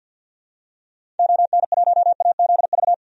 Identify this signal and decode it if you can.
Morse code, heard as a keyed tone, about 51 wpm, 700 Hz. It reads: ON1AZV
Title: DXCC-100 Call Signs - 6 Characters - 50wpm